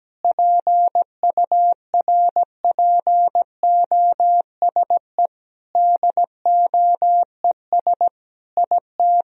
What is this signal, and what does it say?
Morse code, heard as a keyed tone, about 17 wpm, 700 Hz. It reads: PURPOSE DOES IT
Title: Sentences from Top 1000 Words - Review Part 2 - 17wpm